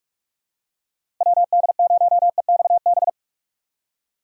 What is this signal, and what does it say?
Morse code, heard as a keyed tone, about 42 wpm, 700 Hz. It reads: WD0EXB
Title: USA Call Signs - 6 Characters - Repeated 2x - 45wpm